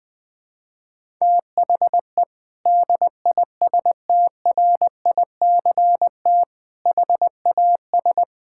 Morse code, heard as a keyed tone, about 19 wpm, 700 Hz. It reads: THE DISTRICT HAS
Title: Sentences from Top 1600 Words 20wpm